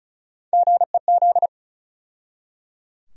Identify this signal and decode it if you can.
Morse code, heard as a keyed tone, about 36 wpm, 700 Hz. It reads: GEZ
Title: USA Call Sign Postfixes - 3 Letters 35wpm